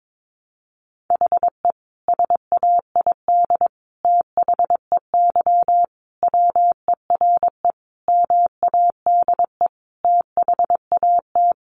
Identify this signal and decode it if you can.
Morse code, heard as a keyed tone, about 22 wpm, 700 Hz. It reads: HE SAID THEY WERE MADE THAT
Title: Sentences from Top 700 Words 22wpm